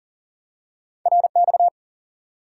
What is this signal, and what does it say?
Morse code, heard as a keyed tone, about 40 wpm, 700 Hz. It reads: RX
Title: Top 90 QSO Elements 40wpm